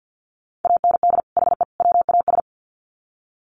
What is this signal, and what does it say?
Morse code, heard as a keyed tone, about 50 wpm, 700 Hz. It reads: AND HE WAS
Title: Sets of 3 Words 50wpm